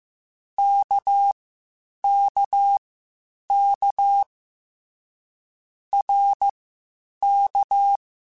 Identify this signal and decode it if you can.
Morse code, heard as a keyed tone, about 15 wpm, spaced as for 8 wpm, 800 Hz. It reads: KKK RK
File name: Short Runs / Short Runs-mp3s / Balanced @13